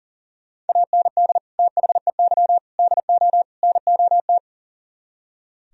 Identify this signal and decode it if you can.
Morse code, heard as a keyed tone, about 51 wpm, 700 Hz. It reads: AND THEY DO NOT